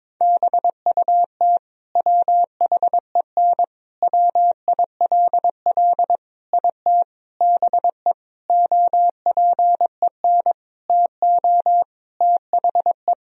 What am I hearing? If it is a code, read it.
Morse code, heard as a keyed tone, about 22 wpm, 700 Hz. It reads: BUT WHEN WILL IT BE OPEN TO THE
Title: Sentences from Top 1000 Words - Review Part 3 - 22wpm